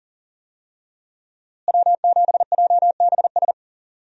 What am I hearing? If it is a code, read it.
Morse code, heard as a keyed tone, about 40 wpm, 700 Hz. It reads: W7JBS
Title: USA Call Signs - 5 Characters - Repeated 2x - 40wpm